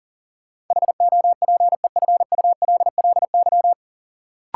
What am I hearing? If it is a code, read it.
Morse code, heard as a keyed tone, about 40 wpm, 700 Hz. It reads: HOPEFULLY
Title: Sentences from Top 1500 Words - New Word Drill Alt - 40wpm